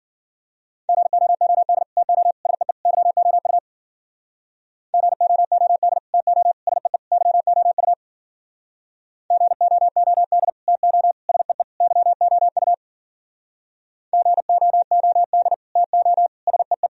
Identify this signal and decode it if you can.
Morse code, heard as a keyed tone, about 53 wpm, 700 Hz. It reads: GOOD TO SEE YOU GOOD TO SEE YOU GOOD TO SEE YOU GOOD TO SEE